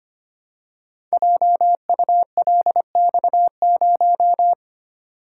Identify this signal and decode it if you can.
Morse code, heard as a keyed tone, about 25 wpm, 700 Hz. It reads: JULX0
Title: Code Groups - 5 Letter:Numbers - 25wpm